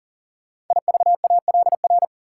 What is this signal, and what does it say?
Morse code, heard as a keyed tone, about 38 wpm, 700 Hz. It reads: IVALR
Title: Code Groups - 5 Letters - 40wpm